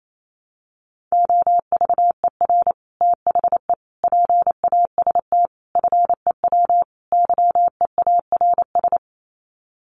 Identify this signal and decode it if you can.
Morse code, heard as a keyed tone, about 28 wpm, 700 Hz. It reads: OVER THE PAST FEW YEARS
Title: Sets of 5 Words Encore 28wpm